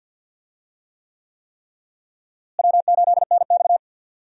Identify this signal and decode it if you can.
Morse code, heard as a keyed tone, about 46 wpm, 700 Hz. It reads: W8NX
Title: USA Call Signs - 4 Characters - Repeated 3x - 50wpm